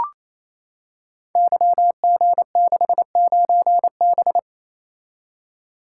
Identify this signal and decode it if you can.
Morse code, heard as a keyed tone, about 29 wpm, 700 Hz. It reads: YG69B
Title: Code Groups - 5 Letter:Numbers - 28wpm